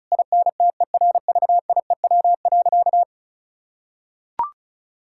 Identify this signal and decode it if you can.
Morse code, heard as a keyed tone, about 34 wpm, 700 Hz. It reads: INTERVIEW.
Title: Sentences from Top 1600 Words 35wpm